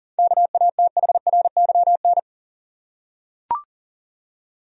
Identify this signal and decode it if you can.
Morse code, heard as a keyed tone, about 40 wpm, 700 Hz. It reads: KATHRYN
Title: USA Common Names - 40wpm